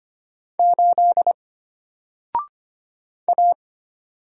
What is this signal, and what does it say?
Morse code, heard as a keyed tone, about 25 wpm, 700 Hz. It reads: 8 A
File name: Single-Letter-Number-25wpm